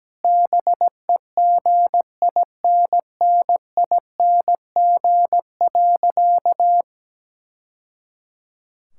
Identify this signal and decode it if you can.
Morse code, heard as a keyed tone, about 17 wpm, 700 Hz. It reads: BEGINNING.
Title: Sentences from Top 1000 Words - Review Part 5 - 17wpm